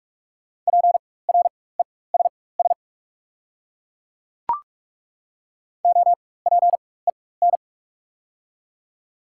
Morse code, heard as a keyed tone, about 43 wpm, spaced as for 21 wpm, 690 Hz. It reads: PRESS OPEN
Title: 4x Character Spacing - Top 300 Words - 45wpm